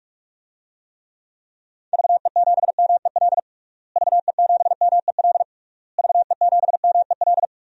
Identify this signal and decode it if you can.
Morse code, heard as a keyed tone, about 41 wpm, 700 Hz. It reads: VE7MEL VE7MEL VE7MEL
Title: DXCC-100 Call Signs - 6 Characters - Repeated 3x - 45wpm